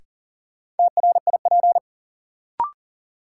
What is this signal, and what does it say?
Morse code, heard as a keyed tone, about 40 wpm, 700 Hz. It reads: TRIP
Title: Sentences from Top 1000 Words - Review Part 4 - New Word Drill - 40wpm